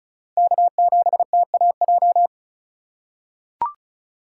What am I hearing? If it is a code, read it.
Morse code, heard as a keyed tone, about 36 wpm, 700 Hz. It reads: K7TAJ
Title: USA Call Signs - 4x Character Spacing - 35wpm